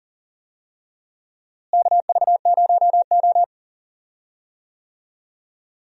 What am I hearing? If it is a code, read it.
Morse code, heard as a keyed tone, about 40 wpm, 700 Hz. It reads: KV0O
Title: USA Call Signs - 4 Characters - Repeated 2x - 40wpm